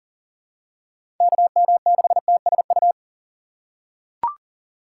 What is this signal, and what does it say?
Morse code, heard as a keyed tone, about 37 wpm, 700 Hz. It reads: KM6TSU
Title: USA Call Signs - 6 Characters - 3x Character Spacing - 40wpm